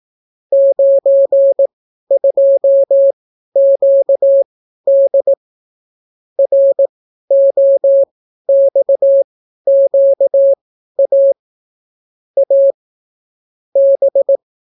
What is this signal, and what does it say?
Morse code, heard as a keyed tone, about 18 wpm, spaced as for 12 wpm, 550 Hz. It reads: 92QD ROXQA A B